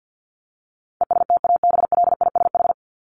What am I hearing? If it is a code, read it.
Morse code, heard as a keyed tone, about 50 wpm, 700 Hz. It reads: ESTABLISH
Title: ED Suffix - Top 100 Words with Base Words 50wpm